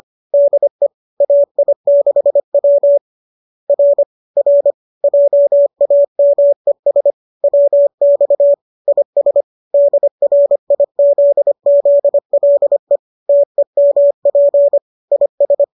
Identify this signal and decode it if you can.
Morse code, heard as a keyed tone, about 25 wpm, 570 Hz. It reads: DE AI6W R R JAMES WX IS DRIZZLE TEMP IS